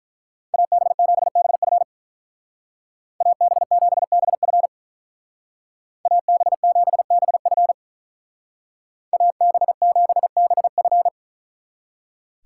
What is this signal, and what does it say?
Morse code, heard as a keyed tone, about 43 wpm, 700 Hz. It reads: AB7BF AB7BF AB7BF AB7BF
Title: Speed Racing - USA 2x2 Call Signs 35wpm